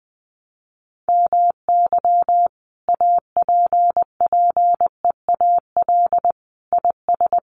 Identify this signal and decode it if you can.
Morse code, heard as a keyed tone, about 20 wpm, 700 Hz. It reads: MY APPEAL IS
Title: Sentences from Top 1300 Words 20wpm